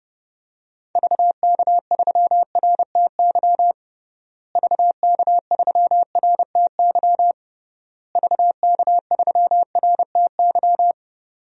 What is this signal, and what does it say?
Morse code, heard as a keyed tone, about 32 wpm, 700 Hz. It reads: VK3RTY VK3RTY VK3RTY
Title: DXCC-100 Call Signs - 6 Characters - Repeated 3x - 30wpm